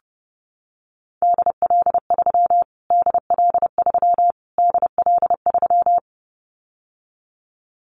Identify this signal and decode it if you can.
Morse code, heard as a keyed tone, about 30 wpm, 700 Hz. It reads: DL3 DL3 DL3
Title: Callsign Prefixes for Top 25 DXCC - Repeated 3x - 30wpm